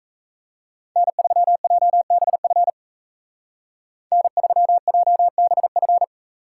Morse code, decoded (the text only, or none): N3JBF N3JBF